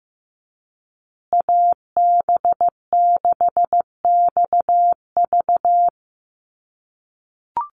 Morse code, heard as a keyed tone, about 15 wpm, 700 Hz. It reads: AB6XV
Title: USA Call Signs Repeated 2x 15wpm